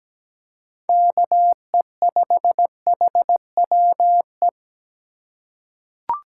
Morse code, heard as a keyed tone, about 17 wpm, 700 Hz. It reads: KE5HWE
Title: USA Call Signs - 6 Characters - 2x Character Spacing - 17wpm